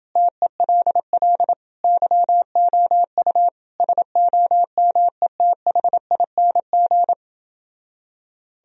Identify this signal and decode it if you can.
Morse code, heard as a keyed tone, about 30 wpm, 700 Hz. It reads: TELL YOU SOMETHING